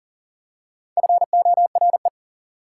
Morse code, heard as a keyed tone, about 40 wpm, 700 Hz. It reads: FORE
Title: Prefixes and Suffixes 40wpm